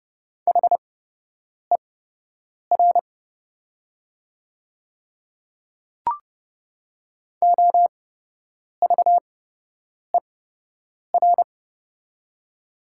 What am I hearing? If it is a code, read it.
Morse code, heard as a keyed tone, about 30 wpm, spaced as for 8 wpm, 700 Hz. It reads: HER OVER
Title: 8x Character Spacing - Top 100 Words - 30wpm